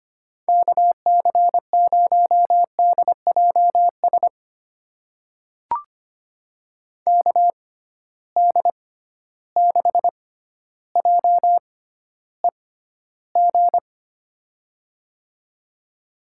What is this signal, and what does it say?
Morse code, heard as a keyed tone, about 25 wpm, spaced as for 13 wpm, 700 Hz. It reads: KC0DJS KD6JEG
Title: USA Call Signs - 6 Characters - 6x Character Spacing - 25wpm